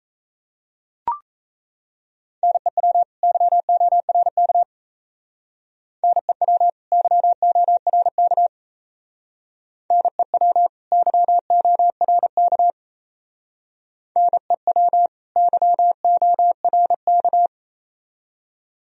Morse code, heard as a keyed tone, about 34 wpm, 700 Hz. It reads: NEW YORK NEW YORK NEW YORK NEW YORK
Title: Speed-Racing - Sets of 2 Words 28wpm